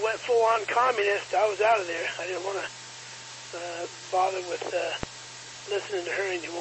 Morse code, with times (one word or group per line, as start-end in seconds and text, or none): none